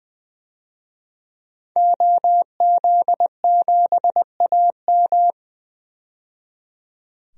Morse code, heard as a keyed tone, about 20 wpm, 700 Hz. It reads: OZ7AM
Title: DXCC-100 Call Signs - 5 Characters - 20wpm